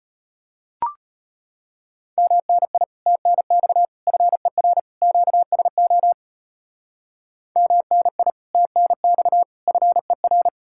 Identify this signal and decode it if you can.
Morse code, heard as a keyed tone, about 37 wpm, 700 Hz. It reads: MNI TNX FER QSO MNI TNX FER